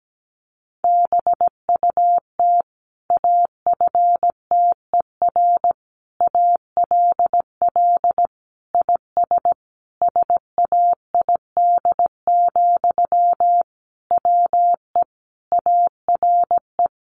Morse code, heard as a keyed tone, about 17 wpm, 700 Hz. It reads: BUT AFTER ALL IS SAID, WE ARE
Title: Sentences from Top 1500 Words 17wpm